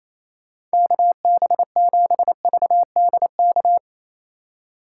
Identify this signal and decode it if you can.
Morse code, heard as a keyed tone, about 34 wpm, 700 Hz. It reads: KB7VDK
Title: Speed Racing - USA 2x3 Call Signs 28wpm